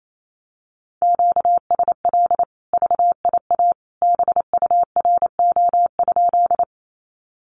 Morse code, heard as a keyed tone, about 28 wpm, 700 Hz. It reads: QSL VIA BURO?
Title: Common QSO Phrases 28wpm